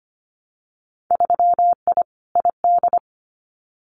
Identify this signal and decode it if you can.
Morse code, heard as a keyed tone, about 25 wpm, 700 Hz. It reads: 3I ID
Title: Winter Field Day Exchanges Encore 25wpm